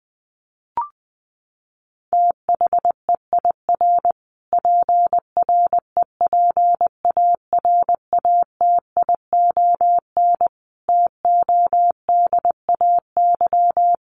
Morse code, heard as a keyed tone, about 20 wpm, 700 Hz. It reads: THEIR PREPARATION TODAY
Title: Sentences from Top 1400 Words 20wpm